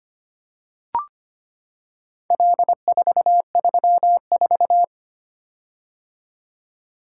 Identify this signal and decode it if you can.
Morse code, heard as a keyed tone, about 25 wpm, 700 Hz. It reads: L434
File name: practice / 25wpm / Four Letter-Number Combinations - 25wpm